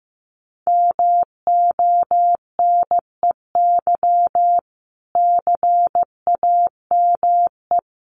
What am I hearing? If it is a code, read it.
Morse code, heard as a keyed tone, about 15 wpm, 700 Hz. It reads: MONEY CAME
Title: Sentences from Top 1100 Words 15wpm